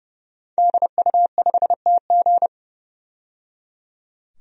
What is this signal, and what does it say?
Morse code, heard as a keyed tone, about 30 wpm, 700 Hz. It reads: DU5TG
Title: DXCC-100 Call Signs - 5 Characters - 30wpm